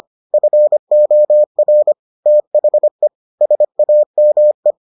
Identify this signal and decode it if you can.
Morse code, heard as a keyed tone, about 25 wpm, 600 Hz. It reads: FOR THE SAME